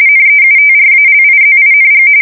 RTTY, about 45.45 baud. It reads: NEUTRALIZING
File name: fsk170 / USS-Massachusetts